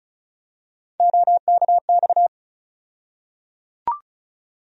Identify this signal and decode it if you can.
Morse code, heard as a keyed tone, about 34 wpm, 700 Hz. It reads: OKX
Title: USA Call Sign Postfixes - 3 Letters 35wpm